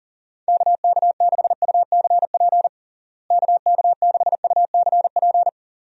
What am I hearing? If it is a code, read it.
Morse code, heard as a keyed tone, about 40 wpm, 700 Hz. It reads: KK6UCP KK6UCP
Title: USA Call Signs - 6 Characters - Repeated 2x - 40wpm